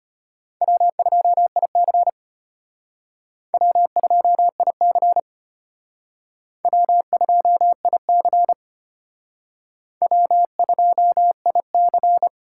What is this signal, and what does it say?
Morse code, heard as a keyed tone, about 33 wpm, 700 Hz. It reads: W2IC W2IC W2IC W2IC